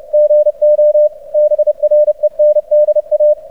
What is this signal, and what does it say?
Morse code, heard as a keyed tone, about 50 wpm, spaced as for 40 wpm, 600 Hz. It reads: GO BRENDA
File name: archived / CRSnet